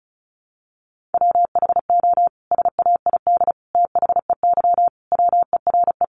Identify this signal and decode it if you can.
Morse code, heard as a keyed tone, about 34 wpm, 700 Hz. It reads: WHO SAID THEY WERE
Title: Sentences from Top 700 Words 35wpm